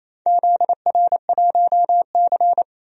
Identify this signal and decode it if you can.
Morse code, heard as a keyed tone, about 28 wpm, 700 Hz. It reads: ZR1C
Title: DXCC-100 Call Signs - 4 Characters - Repeated 3x - 28wpm